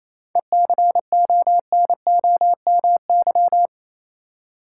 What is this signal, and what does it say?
Morse code, heard as a keyed tone, about 28 wpm, 700 Hz. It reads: ECONOMY
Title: Seven Letter Words - 28wpm